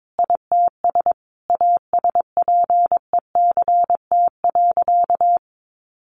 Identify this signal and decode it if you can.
Morse code, heard as a keyed tone, about 22 wpm, 700 Hz. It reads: ITS ASPECT.
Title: Sentences from Top 1300 Words 22wpm